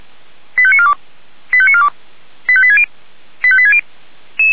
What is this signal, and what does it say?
Five-tone selective call (ZVEI), heard as quick runs of five tones, tones of about 70 ms. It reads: 85942 85942 85860 85860
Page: ZVEI